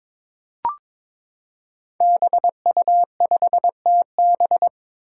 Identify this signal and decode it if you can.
Morse code, heard as a keyed tone, about 23 wpm, 700 Hz. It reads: BU5TB
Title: Five Challenging Letter-Number Combinations 22wpm